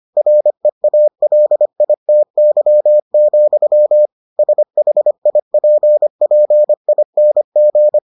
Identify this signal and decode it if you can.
Morse code, heard as a keyed tone, about 25 wpm, 600 Hz. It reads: REALITY, SHIPPING